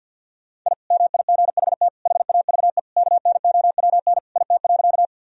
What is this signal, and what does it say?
Morse code, heard as a keyed tone, about 52 wpm, 700 Hz. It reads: I MIGHT HAVE KNOWN IT.
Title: Sentences from Top 1000 Words - Review Part 3 - 50wpm